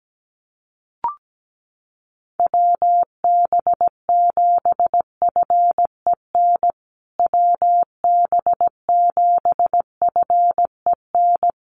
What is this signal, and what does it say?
Morse code, heard as a keyed tone, about 17 wpm, 700 Hz. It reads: WB7FEN WB7FEN
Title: USA Call Signs Repeated 3x 17wpm